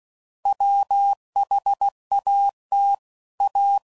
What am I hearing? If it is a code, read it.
Morse code, heard as a keyed tone, about 16 wpm, 800 Hz. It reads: WHAT A
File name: Sayings / Quotes / Deceive